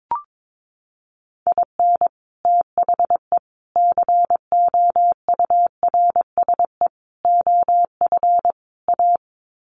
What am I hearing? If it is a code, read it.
Morse code, heard as a keyed tone, about 22 wpm, 700 Hz. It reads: IN THE COURSE OF A
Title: Sets of 5 Words Encore2 22wpm